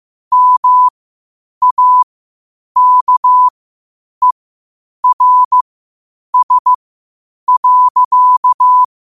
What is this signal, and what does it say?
Morse code, heard as a keyed tone, about 15 wpm, spaced as for 8 wpm, 1000 Hz. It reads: MAKERS.
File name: qotdfiles / 8wpm